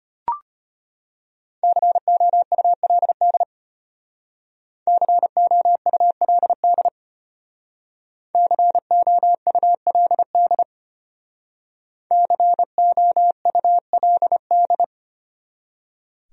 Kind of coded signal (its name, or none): Morse code